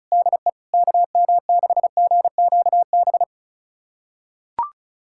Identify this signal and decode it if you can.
Morse code, heard as a keyed tone, about 34 wpm, 700 Hz. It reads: DE KM6GQB
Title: USA Callsign DE Callsign 35wpm